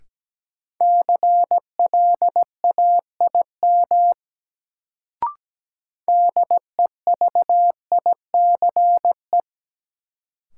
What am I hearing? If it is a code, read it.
Morse code, heard as a keyed tone, about 17 wpm, 700 Hz. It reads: CLAIM DEVICE
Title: Sentences from Top 1500 Words - New Word Drill - 17wpm